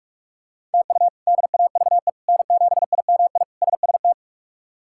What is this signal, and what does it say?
Morse code, heard as a keyed tone, about 46 wpm, 700 Hz. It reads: TU DAVE N7IMI SST
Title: K1USN SST QSO Practice - 45wpm